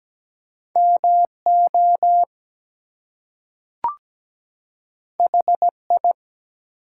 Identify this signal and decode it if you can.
Morse code, heard as a keyed tone, about 17 wpm, 700 Hz. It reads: MO HI
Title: US States and Canadian Provinces 17wpm